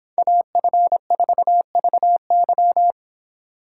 Morse code, heard as a keyed tone, about 26 wpm, 700 Hz. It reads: AF4VY